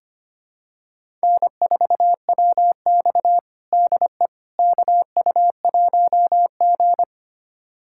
Morse code, heard as a keyed tone, about 25 wpm, 700 Hz. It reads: N4WX DE KU1G
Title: USA Callsign DE Callsign - 4 Characters - 25wpm